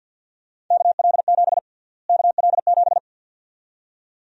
Morse code, heard as a keyed tone, about 51 wpm, 700 Hz. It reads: KL7 KL7
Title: USA Call Sign Prefixes - Repeated 2x - 50wpm